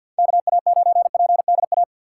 Morse code, heard as a keyed tone, about 52 wpm, 700 Hz. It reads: KA9PDA
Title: USA Call Signs - 4x Character Spacing - 50wpm